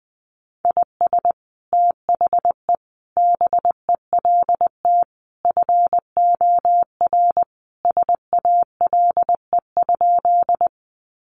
Morse code, heard as a keyed tone, about 20 wpm, 700 Hz. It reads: IS THE BELT FOR SALE?
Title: Sentences from Top 1400 Words 20wpm